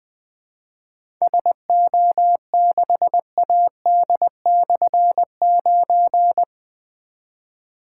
Morse code, heard as a keyed tone, about 20 wpm, 700 Hz. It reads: SO6AD/9
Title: Difficult Call Signs 20wpm